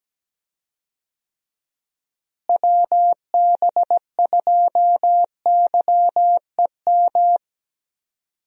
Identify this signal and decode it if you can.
Morse code, heard as a keyed tone, about 17 wpm, 700 Hz. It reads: WB2YEM
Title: USA Call Signs - 6 Characters - Repeated 3x - 17wpm